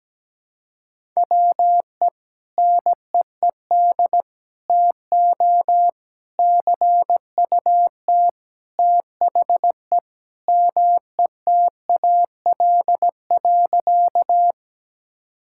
Morse code, heard as a keyed tone, about 17 wpm, 700 Hz. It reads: WE NEED TO CUT THE METAL.
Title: Sentences from Top 1000 Words - Review Part 6 - 17wpm